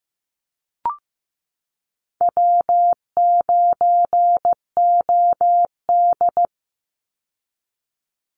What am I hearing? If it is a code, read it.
Morse code, heard as a keyed tone, about 15 wpm, 700 Hz. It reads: W9OD
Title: Call Signs Repeated 3x - 15wpm